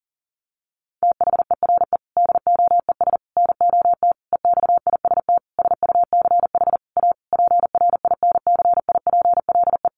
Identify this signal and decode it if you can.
Morse code, heard as a keyed tone, about 37 wpm, 700 Hz. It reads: THERE DOES NOT EXIST SUCH A PRINCIPLE
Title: Sentences from Top 1400 Words 40wpm